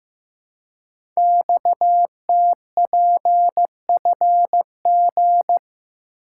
Morse code, heard as a keyed tone, about 15 wpm, 700 Hz. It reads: XTPFG
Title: Code Groups - 5 Letters - 15wpm